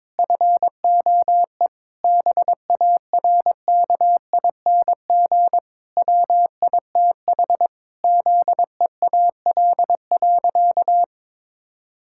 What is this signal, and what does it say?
Morse code, heard as a keyed tone, about 22 wpm, 700 Hz. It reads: FOE BARKING WITH ZEAL.
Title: Pangram Challenge v2 - 22wpm